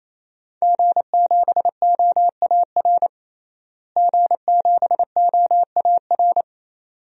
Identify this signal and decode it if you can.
Morse code, heard as a keyed tone, about 28 wpm, 700 Hz. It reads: G7OAR G7OAR
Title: DXCC-100 Call Signs - 5 Characters - Repeated 3x - 28wpm